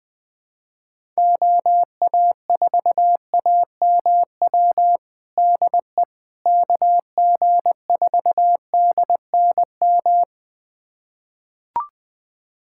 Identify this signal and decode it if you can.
Morse code, heard as a keyed tone, about 20 wpm, 700 Hz. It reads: OA4AMW DE KG4DNM
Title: USA Callsign DE Callsign 20wpm